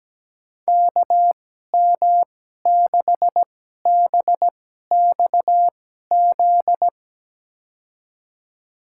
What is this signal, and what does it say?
Morse code, heard as a keyed tone, about 17 wpm, spaced as for 12 wpm, 700 Hz. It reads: KM6BXZ
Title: USA Call Signs - 6 Characters - 2x Character Spacing - 17wpm